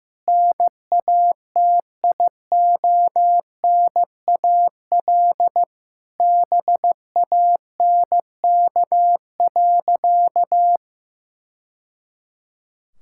Morse code, heard as a keyed tone, about 15 wpm, 700 Hz. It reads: NATIONAL BANK.
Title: Sentences from Top 1000 Words - Review Part 6 - 15wpm